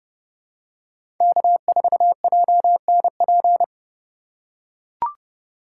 Morse code, heard as a keyed tone, about 31 wpm, 700 Hz. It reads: K4JNP
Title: USA Call Signs - 5 Characters - 2x Character Spacing - 30wpm